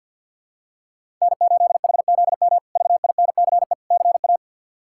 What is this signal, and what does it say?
Morse code, heard as a keyed tone, about 52 wpm, 700 Hz. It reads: N8HZM VINCE CA
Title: K1USN SST QSO Practice - 50wpm